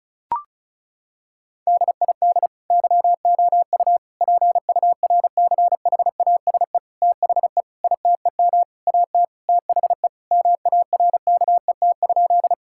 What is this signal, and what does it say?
Morse code, heard as a keyed tone, about 36 wpm, 700 Hz. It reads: DID YOU PURCHASE THE ITEM AT THE MARKET?
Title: Sentences from Top 1000 Words - Review Part 6 - 35wpm